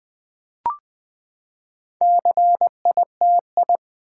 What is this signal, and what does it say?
Morse code, heard as a keyed tone, about 20 wpm, 700 Hz. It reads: CITI
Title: Build A Word - 4 Letters At A Time - 11-Letter Words - 20wpm